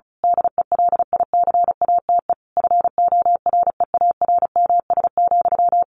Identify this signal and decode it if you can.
Morse code, heard as a keyed tone, about 34 wpm, 700 Hz. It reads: DELICATE FOREARMS,
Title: A Sound Of Thunder 35wpm